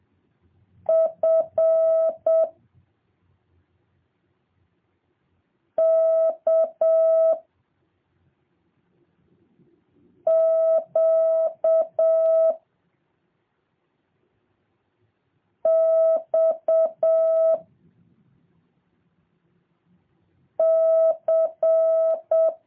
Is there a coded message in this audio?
Yes: Morse code from 0.9 to 22.5 s, heard as a keyed tone, about 7 wpm, 650 Hz. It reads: F K Q X C